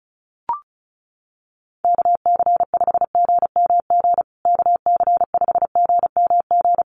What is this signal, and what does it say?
Morse code, heard as a keyed tone, about 34 wpm, 700 Hz. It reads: KC5GMG KC5GMG
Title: USA Call Signs Repeated 3x 35wpm